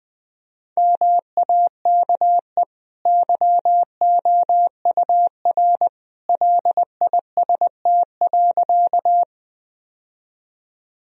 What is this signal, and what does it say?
Morse code, heard as a keyed tone, about 21 wpm, 700 Hz. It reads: MAKE YOUR LIST.
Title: Sentences from Top 1000 Words - Review Part 2 - 20wpm